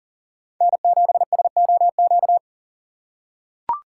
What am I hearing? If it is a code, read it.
Morse code, heard as a keyed tone, about 40 wpm, 700 Hz. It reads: N7SOQ
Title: USA Call Signs - 5 Characters - 6x Character Spacing - 40wpm